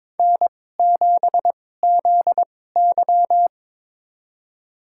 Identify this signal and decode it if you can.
Morse code, heard as a keyed tone, about 22 wpm, spaced as for 16 wpm, 700 Hz. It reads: N7ZY
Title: USA Call Signs - 4 Characters - 2x Character Spacing - 22wpm